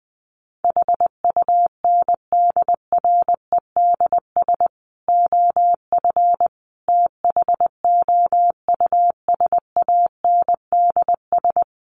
Morse code, heard as a keyed tone, about 19 wpm, 700 Hz. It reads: HUNDREDS OF THOUSANDS
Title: Sets of 5 Words Encore 20wpm